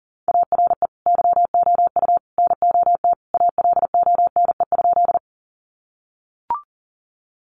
Morse code, heard as a keyed tone, about 39 wpm, 700 Hz. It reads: ARE YOU NOT ALONE?
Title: Sentences from Top 1000 Words 40wpm